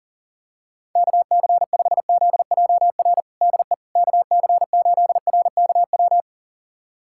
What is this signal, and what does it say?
Morse code, heard as a keyed tone, about 37 wpm, 700 Hz. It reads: KC5ZJR DE KC8RKW
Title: USA Callsign DE Callsign - 6 Characters - 40wpm